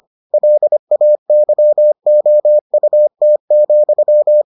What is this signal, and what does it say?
Morse code, heard as a keyed tone, about 25 wpm, 600 Hz. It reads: LAYOUT,